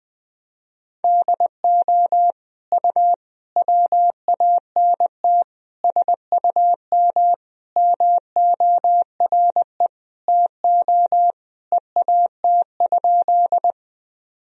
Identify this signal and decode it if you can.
Morse code, heard as a keyed tone, about 20 wpm, 700 Hz. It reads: DO U WANT SUM MORE TO EAT?
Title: Sentences from Top 300 Words and Abbreviations 20wpm